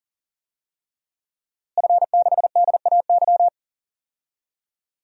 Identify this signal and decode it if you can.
Morse code, heard as a keyed tone, about 37 wpm, 700 Hz. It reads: F6DAY
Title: DXCC-100 Call Signs - 5 Characters - Repeated 3x - 40wpm